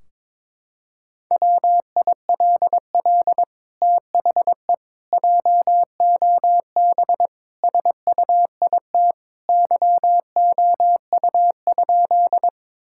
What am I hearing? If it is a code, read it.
Morse code, heard as a keyed tone, about 22 wpm, 700 Hz. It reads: WILL THE JOB SUIT YOU?
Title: Sentences from Top 1600 Words 22wpm